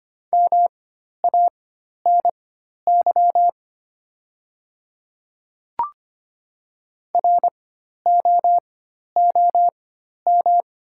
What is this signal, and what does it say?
Morse code, heard as a keyed tone, about 25 wpm, spaced as for 12 wpm, 700 Hz. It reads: MANY ROOM
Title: 4x Character Spacing - Top 500 Words - 25wpm